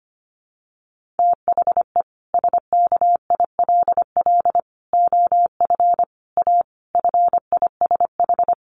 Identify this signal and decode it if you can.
Morse code, heard as a keyed tone, about 25 wpm, 700 Hz. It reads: THE SKILL OF A FISH
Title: Sentences from Top 1400 Words 25wpm